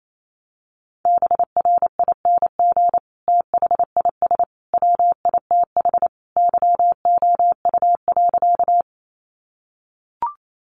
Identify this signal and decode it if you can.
Morse code, heard as a keyed tone, about 28 wpm, 700 Hz. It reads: BRING THIS WITH YOU.
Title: Sentences from Top 500 Words 28wpm